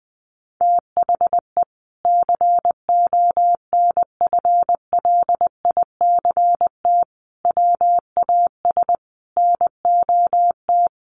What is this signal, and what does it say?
Morse code, heard as a keyed tone, about 20 wpm, 700 Hz. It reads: THE CONFLICT WAS NOT
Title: Sentences from Top 1300 Words 20wpm